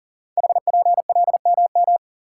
Morse code, heard as a keyed tone, about 39 wpm, 700 Hz. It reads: HPLMM
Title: Code Groups - 5 Letters - 40wpm